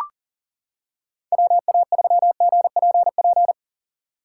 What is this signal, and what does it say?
Morse code, heard as a keyed tone, about 39 wpm, 700 Hz. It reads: WA3GPP
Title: K1USN SST QSO Practice - 40wpm